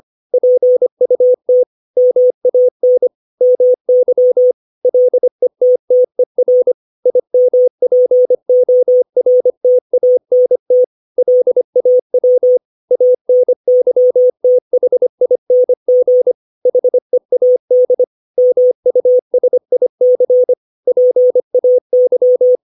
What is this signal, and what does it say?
Morse code, heard as a keyed tone, about 25 wpm, 500 Hz. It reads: PUT MAN MY LETTER IMPORTANT LAW ANYTHING HEAD MUSIC PAY
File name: 632 words / 632 Words 25WPM